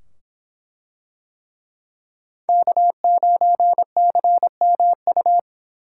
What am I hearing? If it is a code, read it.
Morse code, heard as a keyed tone, about 24 wpm, 700 Hz. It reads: K9CMU